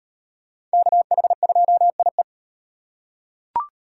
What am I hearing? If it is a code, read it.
Morse code, heard as a keyed tone, about 38 wpm, 700 Hz. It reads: KH2IE